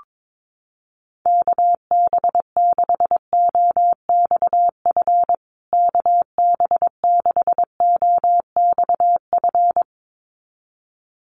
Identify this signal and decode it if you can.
Morse code, heard as a keyed tone, about 22 wpm, 700 Hz. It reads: KB6OXF KB6OXF
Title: USA Call Signs Repeated 2x 22wpm